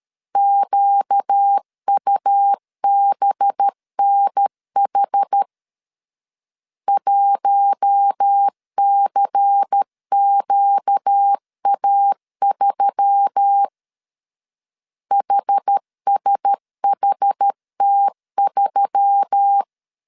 Morse code, heard as a keyed tone, about 13 wpm, 800 Hz. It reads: QUBNH 1CQA3 HSHT3
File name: CW / DET / 1 6 1500C S A 12wpm 3440